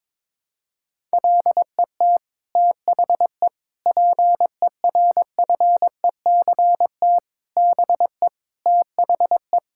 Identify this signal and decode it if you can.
Morse code, heard as a keyed tone, about 22 wpm, 700 Hz. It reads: LET THE PERFECT BE THE
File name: Sets of 9 Words - 22wpm